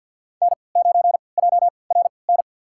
Morse code, heard as a keyed tone, about 52 wpm, spaced as for 29 wpm, 700 Hz. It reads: N9JRN
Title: USA Call Signs - 5 Characters - 3x Character Spacing - 50wpm